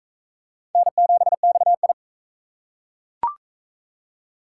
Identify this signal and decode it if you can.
Morse code, heard as a keyed tone, about 39 wpm, 700 Hz. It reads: N7XI